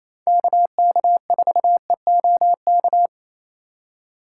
Morse code, heard as a keyed tone, about 28 wpm, 700 Hz. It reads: KK4EOK
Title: USA Call Signs - 6 Characters - 4x Character Spacing - 28wpm